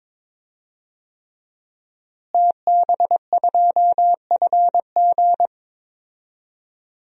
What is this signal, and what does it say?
Morse code, heard as a keyed tone, about 22 wpm, 700 Hz. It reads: TB2FG
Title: DXCC-100 Call Signs - 5 Characters - Repeated 3x - 22wpm